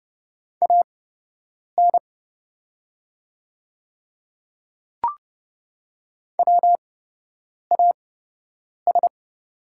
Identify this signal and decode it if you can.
Morse code, heard as a keyed tone, about 30 wpm, spaced as for 8 wpm, 700 Hz. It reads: AN WAS
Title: 8x Character Spacing - Top 100 Words - 30wpm